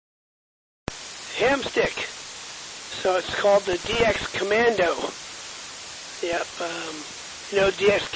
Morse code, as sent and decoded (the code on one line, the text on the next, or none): none